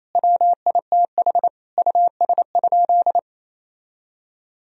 Morse code, heard as a keyed tone, about 28 wpm, 700 Hz. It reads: WITH US?
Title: Sentences from Top 1000 Words - Review Part 1 - 28wpm